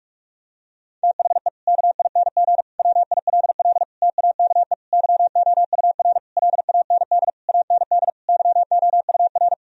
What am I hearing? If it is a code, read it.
Morse code, heard as a keyed tone, about 45 wpm, 700 Hz. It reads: THE KING WILL TAKE YOUR LAND AND YOUR
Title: Sentences from Top 1000 Words - Review Part 4 - 45wpm